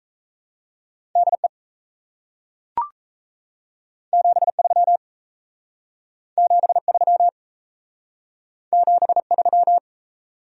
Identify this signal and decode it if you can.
Morse code, heard as a keyed tone, about 37 wpm, 700 Hz. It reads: DE 73 73 73